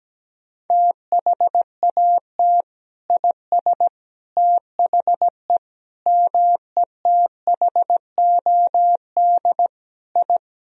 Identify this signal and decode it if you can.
Morse code, heard as a keyed tone, about 17 wpm, 700 Hz. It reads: THAT IS THE METHOD I
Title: Sentences from Top 1000 Words - Review Part 3 - 17wpm